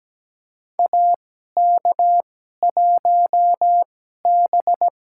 Morse code, heard as a keyed tone, about 17 wpm, spaced as for 12 wpm, 700 Hz. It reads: AK1B
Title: USA Call Signs - 2x Character Spacing - 17wpm